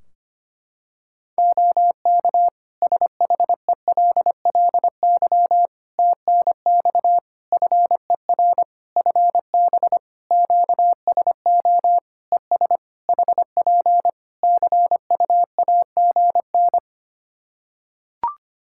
Morse code, heard as a keyed tone, about 25 wpm, 700 Hz. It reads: OK SHELLY TNX FER FB QSO ES HP CUAGN
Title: Practice QSO - Exchange 4 - Ending - 25wpm